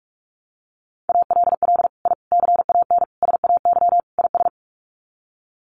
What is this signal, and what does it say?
Morse code, heard as a keyed tone, about 44 wpm, 700 Hz. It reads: ALL I CAN SAY IS
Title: Sets of 5 Words Encore2 45wpm